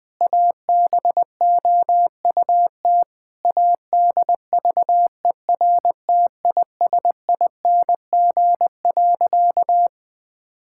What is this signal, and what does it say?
Morse code, heard as a keyed tone, about 20 wpm, 700 Hz. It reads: ABOUT ADVERTISING.
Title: Sentences from Top 1000 Words - Review Part 6 - 20wpm